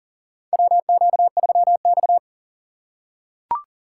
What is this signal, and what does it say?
Morse code, heard as a keyed tone, about 40 wpm, 700 Hz. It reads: WQ3X
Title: USA Call Signs - 4 Characters - 6x Character Spacing - 40wpm